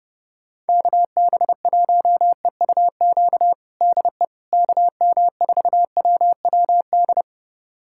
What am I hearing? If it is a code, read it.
Morse code, heard as a keyed tone, about 30 wpm, 700 Hz. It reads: KB1EUQ DE KM4WWD
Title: USA Callsign DE Callsign - 6 Characters - 30wpm